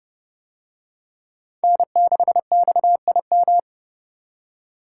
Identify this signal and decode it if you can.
Morse code, heard as a keyed tone, about 29 wpm, 700 Hz. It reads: N6XIM